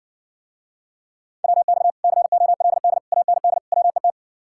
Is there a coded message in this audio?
Yes: Morse code from 1.4 to 4.1 s, heard as a keyed tone, about 69 wpm, 700 Hz. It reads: WX COLD AND WET